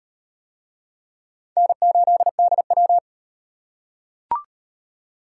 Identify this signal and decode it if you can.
Morse code, heard as a keyed tone, about 38 wpm, 700 Hz. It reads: N8DW